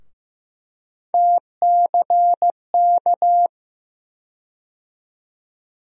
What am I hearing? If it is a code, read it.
Morse code, heard as a keyed tone, about 15 wpm, 700 Hz. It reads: TCK
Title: USA Call Sign Postfixes - 3 Letters 15wpm